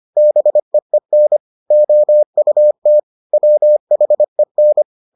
Morse code, heard as a keyed tone, about 25 wpm, 600 Hz. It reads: BEEN OUT WHEN